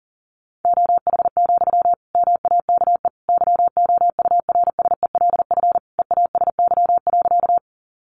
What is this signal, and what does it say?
Morse code, heard as a keyed tone, about 40 wpm, 700 Hz. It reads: OH, MAKE YOURSELF EASY.
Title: Sentences from Top 700 Words 40wpm